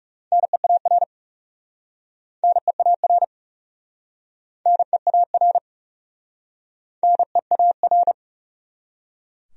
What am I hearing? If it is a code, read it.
Morse code, heard as a keyed tone, about 40 wpm, 700 Hz. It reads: NEAR NEAR NEAR NEAR